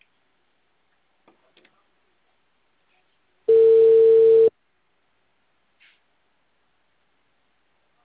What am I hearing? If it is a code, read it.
Ringback tone (China, 450 Hz).
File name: out- -2005- - - .5582